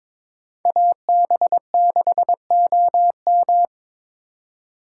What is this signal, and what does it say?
Morse code, heard as a keyed tone, about 22 wpm, 700 Hz. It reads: AB6OM
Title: USA Call Signs - 5 Characters - 2x Character Spacing - 22wpm